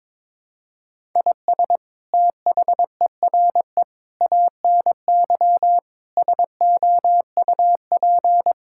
Morse code, heard as a keyed tone, about 22 wpm, 700 Hz. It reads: IS THERE ANY SOUP
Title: Sentences from Top 1000 Words - Review Part 6 - 22wpm